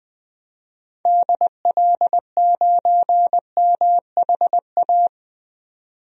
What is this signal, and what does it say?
Morse code, heard as a keyed tone, about 20 wpm, 700 Hz. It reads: DL9MHA
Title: DXCC-100 Call Signs - 6 Characters - 20wpm